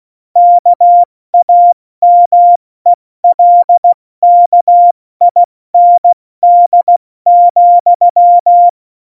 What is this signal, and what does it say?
Morse code, heard as a keyed tone, about 16 wpm, spaced as for 14 wpm, 700 Hz. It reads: KAMELKIND,
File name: Morsekurs / Witze / Witz 06. November 2018, 70 F80